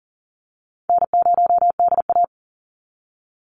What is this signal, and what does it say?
Morse code, heard as a keyed tone, about 40 wpm, 700 Hz. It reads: N0DA
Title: NA SOTA Chaser Callsigns 40wpm